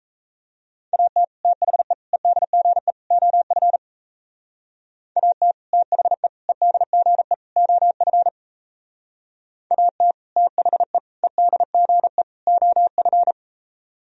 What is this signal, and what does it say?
Morse code, heard as a keyed tone, about 37 wpm, 700 Hz. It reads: AT THE EDGE OF AT THE EDGE OF AT THE EDGE OF